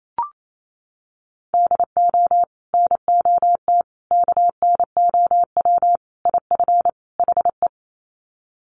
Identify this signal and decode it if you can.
Morse code, heard as a keyed tone, about 28 wpm, 700 Hz. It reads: DO NOT KNOW IF HE
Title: Sets of 5 Words Encore2 28wpm